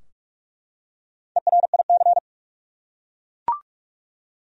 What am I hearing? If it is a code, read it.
Morse code, heard as a keyed tone, about 41 wpm, 700 Hz. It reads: ERIC
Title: USA Common Names - Max 4 Letters - 45wpm